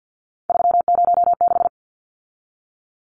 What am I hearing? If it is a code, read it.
Morse code, heard as a keyed tone, about 50 wpm, 700 Hz. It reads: 306
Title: Numbers - 3-Digit 50wpm